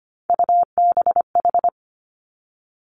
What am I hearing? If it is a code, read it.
Morse code, heard as a keyed tone, about 25 wpm, 700 Hz. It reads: UBH
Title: Three Challenging Letter-Number Combinations 25wpm